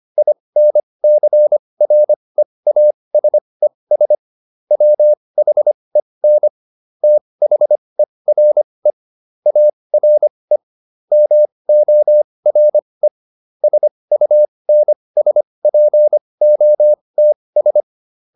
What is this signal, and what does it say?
Morse code, heard as a keyed tone, about 25 wpm, spaced as for 20 wpm, 600 Hz. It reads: INCREASES WHEN THERE ARE MORE SUNSPOTS